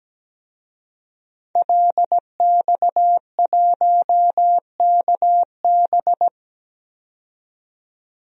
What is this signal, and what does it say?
Morse code, heard as a keyed tone, about 17 wpm, 700 Hz. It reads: LX1KB
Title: DXCC-100 Call Signs - 5 Characters - 17wpm